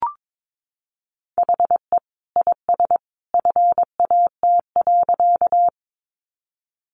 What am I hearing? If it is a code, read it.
Morse code, heard as a keyed tone, about 22 wpm, 700 Hz. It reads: HE IS FAT.
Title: Easy Phrases 22wpm